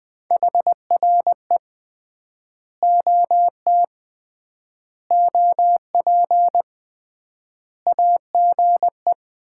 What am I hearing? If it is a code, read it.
Morse code, heard as a keyed tone, about 21 wpm, 700 Hz. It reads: HRE OT OP AGE
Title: Common QSO Phrases - 3x Word-Spacing - 20wpm